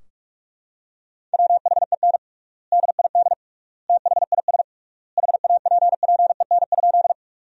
Morse code, heard as a keyed tone, about 46 wpm, 700 Hz. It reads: WHEN DID THIS HAPPEN?
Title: 3x Word-Spacing - Sentences from Top 400 Words - 45wpm